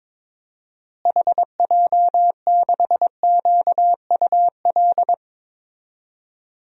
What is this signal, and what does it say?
Morse code, heard as a keyed tone, about 21 wpm, 700 Hz. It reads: HJ6QUL
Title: DXCC-100 Call Signs - 6 Characters - Repeated 3x - 22wpm